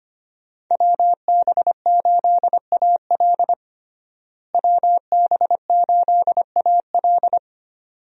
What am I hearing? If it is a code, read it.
Morse code, heard as a keyed tone, about 25 wpm, 700 Hz. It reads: WB8AL WB8AL
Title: USA Call Signs - 5 Characters - Repeated 2x - 25wpm